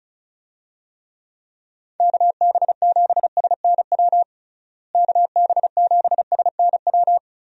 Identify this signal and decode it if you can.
Morse code, heard as a keyed tone, about 34 wpm, 700 Hz. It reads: KB7SNW KB7SNW
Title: USA Call Signs - 6 Characters - Repeated 3x - 35wpm